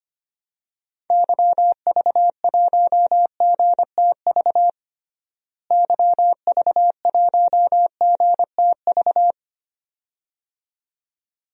Morse code, heard as a keyed tone, about 25 wpm, 700 Hz. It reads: YV1GTV YV1GTV
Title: DXCC-100 Call Signs - 6 Characters - Repeated 2x - 25wpm